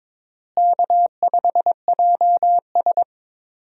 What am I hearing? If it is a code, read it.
Morse code, heard as a keyed tone, about 22 wpm, 700 Hz. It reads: K5JS
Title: USA Call Signs - 4 Characters - Repeated 3x - 22wpm